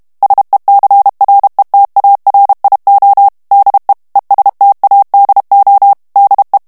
Morse code, heard as a keyed tone, about 32 wpm, 800 Hz. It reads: SECRETARIO DE ESTADO DE
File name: ESPAGNOL TEXTE EN CLAIR 2 VITESSE 30 MOTS MINUTES